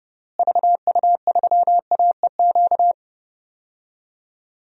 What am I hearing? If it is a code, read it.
Morse code, heard as a keyed tone, about 32 wpm, 700 Hz. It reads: VU3AEQ
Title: DXCC-100 Call Signs - 6 Characters - Repeated 3x - 30wpm